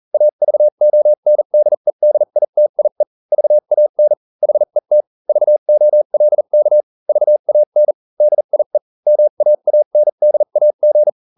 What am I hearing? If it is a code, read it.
Morse code, heard as a keyed tone, about 40 wpm, spaced as for 35 wpm, 600 Hz. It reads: AVONDEDITIE VAN HET VOLK VAN DIE MAANDAG